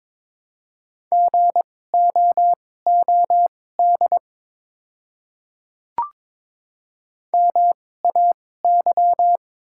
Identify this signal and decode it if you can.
Morse code, heard as a keyed tone, about 22 wpm, spaced as for 16 wpm, 700 Hz. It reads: GOOD MAY
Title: 2x Character Spacing - Top 200 Words - 22wpm